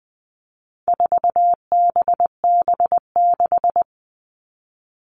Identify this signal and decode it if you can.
Morse code, heard as a keyed tone, about 20 wpm, 700 Hz. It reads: 4BB6
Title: All Cut Numbers - 4-Digit 20wpm